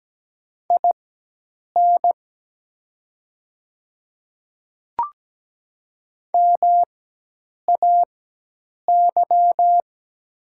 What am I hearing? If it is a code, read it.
Morse code, heard as a keyed tone, about 17 wpm, spaced as for 8 wpm, 700 Hz. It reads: IN MAY
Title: 4x Character Spacing - Top 100 Words - 17wpm